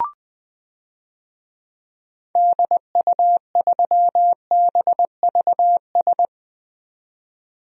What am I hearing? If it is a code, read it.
Morse code, heard as a keyed tone, about 20 wpm, 700 Hz. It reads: DU3BVS
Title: DXCC-100 Call Signs - 6 Characters - 20wpm